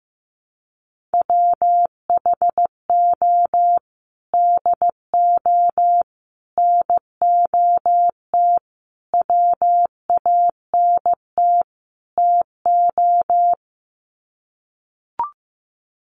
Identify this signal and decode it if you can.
Morse code, heard as a keyed tone, about 15 wpm, 700 Hz. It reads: WHO DO NOT WANT TO
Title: Sets of 5 Words Encore2 15wpm